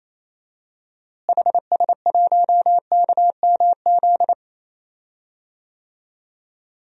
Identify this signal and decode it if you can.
Morse code, heard as a keyed tone, about 28 wpm, 700 Hz. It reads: HS1KMZ
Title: DXCC-100 Call Signs - 6 Characters - Repeated 2x - 28wpm